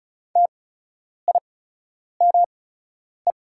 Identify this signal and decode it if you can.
Morse code, heard as a keyed tone, about 36 wpm, spaced as for 10 wpm, 700 Hz. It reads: TIME
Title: 8x Character Spacing - Top 100 Words - 35wpm